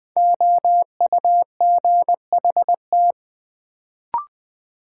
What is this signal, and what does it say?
Morse code, heard as a keyed tone, about 20 wpm, 700 Hz. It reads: OUGHT
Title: Sets of 5 Words Encore - New Word Drill - 20wpm